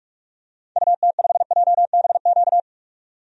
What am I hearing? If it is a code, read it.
Morse code, heard as a keyed tone, about 44 wpm, 700 Hz. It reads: UT5JBQ
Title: DXCC-100 Call Signs - 6 Characters - 45wpm